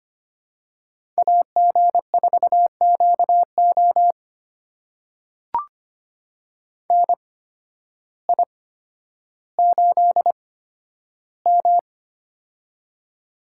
Morse code, heard as a keyed tone, about 25 wpm, spaced as for 12 wpm, 700 Hz. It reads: AG4QO NI8M
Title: USA Call Signs - 8x Character Spacing - 25wpm